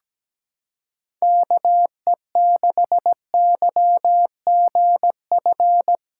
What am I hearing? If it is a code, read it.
Morse code, heard as a keyed tone, about 17 wpm, 700 Hz. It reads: KE6YGF
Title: USA Callsign DE Callsign - 6 Characters - 17wpm